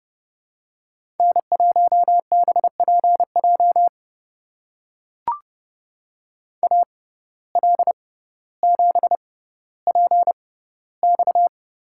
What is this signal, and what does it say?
Morse code, heard as a keyed tone, about 31 wpm, spaced as for 15 wpm, 700 Hz. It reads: N1BPJ AL7PX
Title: USA Call Signs - 5 Characters - 6x Character Spacing - 30wpm